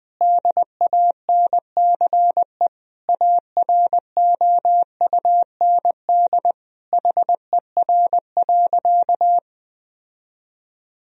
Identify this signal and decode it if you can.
Morse code, heard as a keyed tone, about 20 wpm, 700 Hz. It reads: DANCE AROUND HER.
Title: Sentences from Top 1000 Words - Review Part 4 - 20wpm